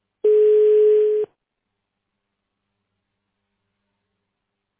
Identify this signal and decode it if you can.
Ringback tone (Europe, 425 Hz).